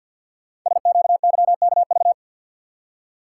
Morse code, heard as a keyed tone, about 45 wpm, 700 Hz. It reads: SQYKV
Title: Code Groups - 5 Letters - 50wpm